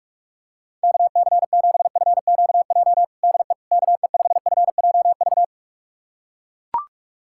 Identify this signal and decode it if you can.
Morse code, heard as a keyed tone, about 43 wpm, 700 Hz. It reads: KC7FQJ DE KE5FJV
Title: USA Callsign DE Callsign - 6 Characters - 45wpm